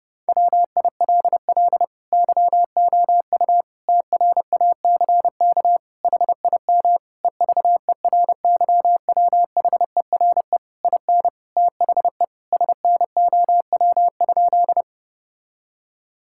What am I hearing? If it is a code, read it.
Morse code, heard as a keyed tone, about 28 wpm, 700 Hz. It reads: WILL YOU TRACK HIM EVERYWHERE IN THE SNOW?
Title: Sentences from Top 1600 Words 30wpm